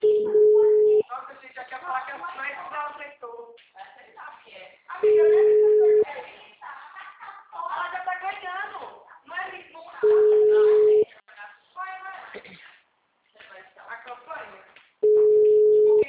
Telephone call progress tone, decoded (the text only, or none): ringback tone